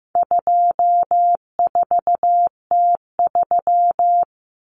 Morse code, heard as a keyed tone, about 15 wpm, 700 Hz. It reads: 24T3
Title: Common Cut Numbers - 4-Digit 15wpm